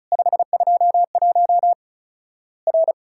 Morse code, heard as a keyed tone, about 36 wpm, 700 Hz. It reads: 521 R
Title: Worked All Europe DX Contest - QTC Exchange - 35wpm